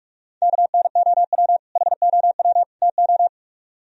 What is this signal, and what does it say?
Morse code, heard as a keyed tone, about 57 wpm, 700 Hz. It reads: KNOW HOW TO